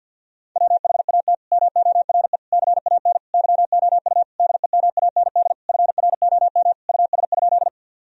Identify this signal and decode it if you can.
Morse code, heard as a keyed tone, about 50 wpm, 700 Hz. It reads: WHAT MORE CAN YOU DEMAND FROM US?
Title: Sentences from Top 1000 Words - Review Part 4 - 50wpm